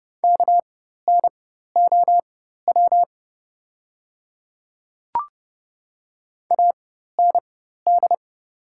Morse code, heard as a keyed tone, about 30 wpm, spaced as for 14 wpm, 700 Hz. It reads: KNOW AND
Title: 4x Character Spacing - Top 200 Words - 30wpm